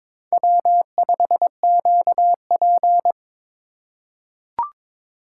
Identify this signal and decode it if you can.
Morse code, heard as a keyed tone, about 22 wpm, 700 Hz. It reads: W5QP
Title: K1USN SST QSO Practice - 22wpm